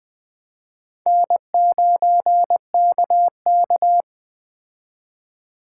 Morse code, heard as a keyed tone, about 21 wpm, 700 Hz. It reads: N9KK